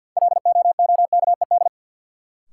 Morse code, heard as a keyed tone, about 53 wpm, 700 Hz. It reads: LOOKED
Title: Sentences from Top 800 Words - New Word Drill - 50wpm